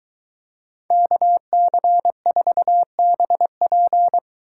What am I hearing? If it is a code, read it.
Morse code, heard as a keyed tone, about 22 wpm, 700 Hz. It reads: KC4BP